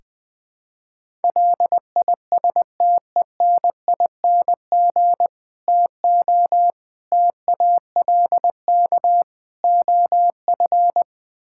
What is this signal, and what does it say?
Morse code, heard as a keyed tone, about 20 wpm, 700 Hz. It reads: LISTENING TO TALK OF
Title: Build A Phrase - 5-Word Phrases - 20wpm